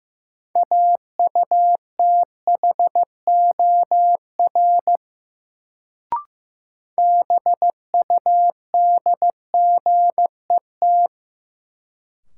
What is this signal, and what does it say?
Morse code, heard as a keyed tone, about 16 wpm, 700 Hz. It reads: AUTHOR BUDGET
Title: Sentences from Top 1200 Words - New Word Drill - 15wpm